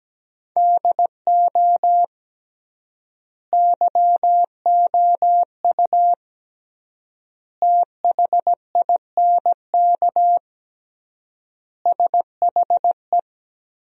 Morse code, heard as a keyed tone, about 17 wpm, 700 Hz. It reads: DO YOU THINK SHE
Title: 3x Word-Spacing - Sentences from Top 200 Words - 17wpm